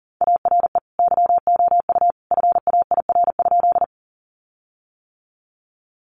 Morse code, heard as a keyed tone, about 40 wpm, 700 Hz. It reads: ARE YOU FAIR?
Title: Sentences from Top 1300 Words 40wpm